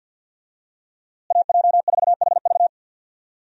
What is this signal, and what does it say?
Morse code, heard as a keyed tone, about 46 wpm, 700 Hz. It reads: AJ4HV
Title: USA Call Signs - 5 Characters - Repeated 3x - 50wpm